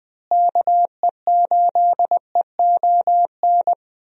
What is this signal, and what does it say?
Morse code, heard as a keyed tone, about 21 wpm, 700 Hz. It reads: KE8EON
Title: USA Call Signs - 6 Characters - Repeated 2x - 20wpm